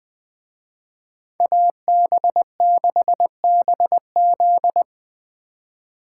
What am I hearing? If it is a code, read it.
Morse code, heard as a keyed tone, about 24 wpm, 700 Hz. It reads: AB6BZ
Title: Speed Racing - USA 2x2 Call Signs 20wpm